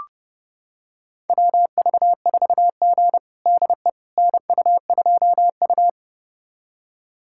Morse code, heard as a keyed tone, about 32 wpm, 700 Hz. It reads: WV4G DE NU2U
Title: USA Callsign DE Callsign - 4 Characters - 30wpm